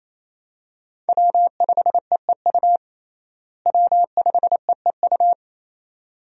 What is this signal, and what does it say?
Morse code, heard as a keyed tone, about 28 wpm, 700 Hz. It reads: W5EEU W5EEU
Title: USA Call Signs - 5 Characters - Repeated 2x - 28wpm